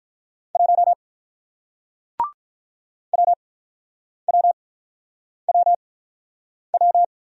Morse code, heard as a keyed tone, about 47 wpm, spaced as for 9 wpm, 700 Hz. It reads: 1 WWWW